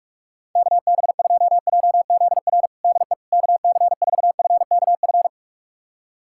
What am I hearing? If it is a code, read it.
Morse code, heard as a keyed tone, about 41 wpm, 700 Hz. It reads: KB2JZR DE KC4FKF
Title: USA Callsign DE Callsign - 6 Characters - 45wpm